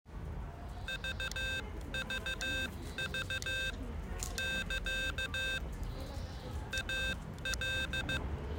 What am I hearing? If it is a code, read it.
Morse code, heard as a keyed tone, about 15 wpm, spaced as for 13 wpm, 1550 Hz. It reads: VVV <KA> AL